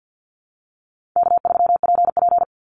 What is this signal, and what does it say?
Morse code, heard as a keyed tone, about 50 wpm, 700 Hz. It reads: K3PP
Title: Easy Call Signs - 50wpm